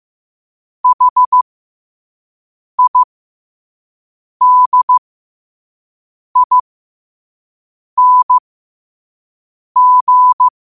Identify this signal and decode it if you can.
Morse code, heard as a keyed tone, about 15 wpm, spaced as for 5 wpm, 1000 Hz. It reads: HIDING
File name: qotdfiles / 5wpm